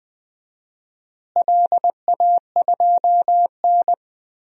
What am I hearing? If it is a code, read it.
Morse code, heard as a keyed tone, about 21 wpm, 700 Hz. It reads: LA2N
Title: DXCC-100 Call Signs - 4 Characters - Repeated 2x - 20wpm